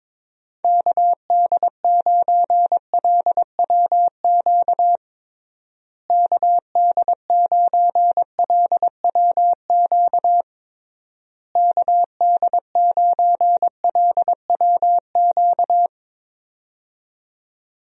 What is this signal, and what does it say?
Morse code, heard as a keyed tone, about 22 wpm, 700 Hz. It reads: KD9LWQ KD9LWQ KD9LWQ
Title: USA Call Signs - 6 Characters - Repeated 3x - 22wpm